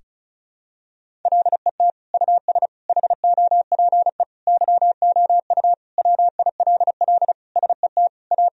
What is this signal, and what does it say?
Morse code, heard as a keyed tone, about 36 wpm, 690 Hz. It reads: LET US HOPE YOU WILL SET A
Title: Sentences from Top 1000 Words - Review Part 1 - 35wpm